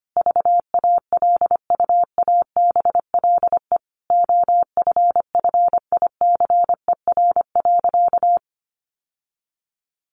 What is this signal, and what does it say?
Morse code, heard as a keyed tone, about 25 wpm, 700 Hz. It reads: VALUABLE OFFICER.
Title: Sentences from Top 1200 Words 25wpm